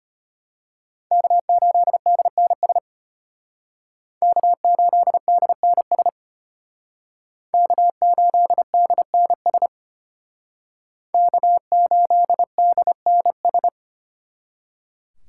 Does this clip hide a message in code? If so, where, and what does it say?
Yes: Morse code from 1.1 to 13.7 s, heard as a keyed tone, about 32 wpm, 700 Hz. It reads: K8DNS K8DNS K8DNS K8DNS